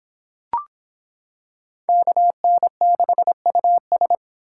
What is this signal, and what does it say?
Morse code, heard as a keyed tone, about 25 wpm, 700 Hz. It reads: KN6US